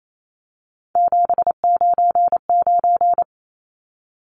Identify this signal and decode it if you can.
Morse code, heard as a keyed tone, about 28 wpm, 700 Hz. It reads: 799
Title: Numbers - 3-Digit 28wpm